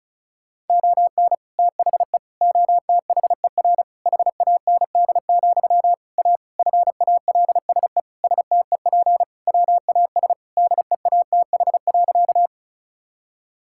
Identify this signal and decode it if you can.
Morse code, heard as a keyed tone, about 36 wpm, 700 Hz. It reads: ON THE OTHER HAND, A FALSE STEP WAS DEATH.
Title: Sentences from Top 1600 Words 35wpm